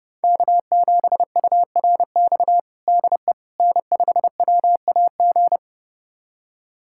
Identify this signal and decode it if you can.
Morse code, heard as a keyed tone, about 32 wpm, 700 Hz. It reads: K7URX DE N5WAG
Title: USA Callsign DE Callsign - 5 Characters - 30wpm